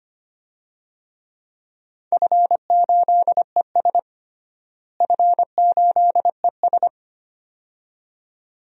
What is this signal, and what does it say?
Morse code, heard as a keyed tone, about 25 wpm, 700 Hz. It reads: F8ES F8ES
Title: DXCC-100 Call Signs - 4 Characters - Repeated 2x - 25wpm